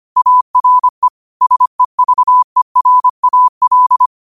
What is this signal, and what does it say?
Morse code, heard as a keyed tone, about 25 wpm, 1000 Hz. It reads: ARE SEVERAL